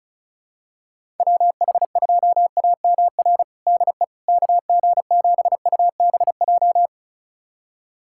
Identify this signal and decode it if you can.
Morse code, heard as a keyed tone, about 34 wpm, 700 Hz. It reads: WH2AMR DE KG7UBJ
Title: USA Callsign DE Callsign - 6 Characters - 35wpm